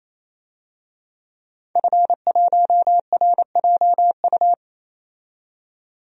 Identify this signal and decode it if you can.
Morse code, heard as a keyed tone, about 28 wpm, 700 Hz. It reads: F1RJU
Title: DXCC-100 Call Signs - 5 Characters - 28wpm